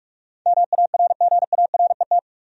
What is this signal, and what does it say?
Morse code, heard as a keyed tone, about 47 wpm, 700 Hz. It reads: MARGARET
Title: USA Common Names - 45wpm